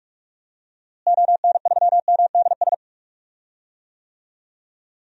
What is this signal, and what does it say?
Morse code, heard as a keyed tone, about 42 wpm, 700 Hz. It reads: ON3MDS
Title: DXCC-100 Call Signs - 6 Characters - Repeated 2x - 45wpm